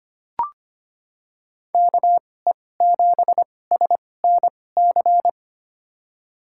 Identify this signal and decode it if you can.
Morse code, heard as a keyed tone, about 25 wpm, spaced as for 18 wpm, 700 Hz. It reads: KE7SNC
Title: USA Call Signs - 6 Characters - 2x Character Spacing - 25wpm